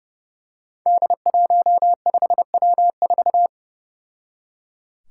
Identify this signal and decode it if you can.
Morse code, heard as a keyed tone, about 30 wpm, 700 Hz. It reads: D15W4
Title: Five Challenging Letter-Number Combinations 30wpm